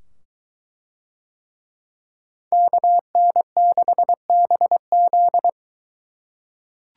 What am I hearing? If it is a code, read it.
Morse code, heard as a keyed tone, about 22 wpm, 700 Hz. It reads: KN6BZ